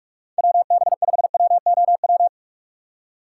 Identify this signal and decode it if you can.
Morse code, heard as a keyed tone, about 45 wpm, 700 Hz. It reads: WB5WOW